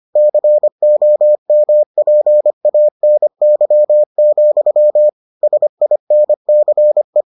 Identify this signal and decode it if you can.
Morse code, heard as a keyed tone, about 25 wpm, 600 Hz. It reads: COMPANY, SINCE